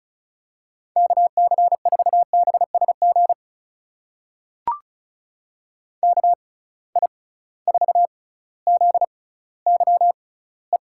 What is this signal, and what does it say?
Morse code, heard as a keyed tone, about 34 wpm, spaced as for 18 wpm, 700 Hz. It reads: KC4BSG KI4ZYE
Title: USA Call Signs - 6 Characters - 6x Character Spacing - 35wpm